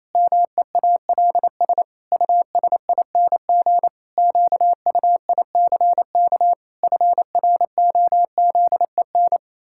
Morse code, heard as a keyed tone, about 28 wpm, 700 Hz. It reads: MEALS USING QUICK FROZEN
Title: Pangram Challenge v2 - 28wpm